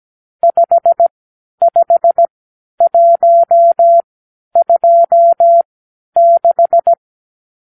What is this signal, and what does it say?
Morse code, heard as a keyed tone, about 17 wpm, spaced as for 10 wpm, 680 Hz. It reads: 55126
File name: k7qo / Lesson 52 (8)